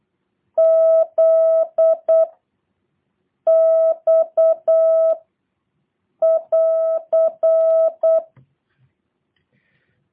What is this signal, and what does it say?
Morse code, heard as a keyed tone, about 8 wpm, 650 Hz. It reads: Z X <AR>